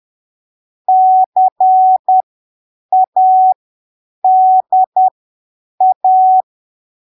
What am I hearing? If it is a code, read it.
Morse code, heard as a keyed tone, about 10 wpm, spaced as for 7 wpm, 750 Hz. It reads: CADA
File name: arquivos / texto / texto 19 10ppm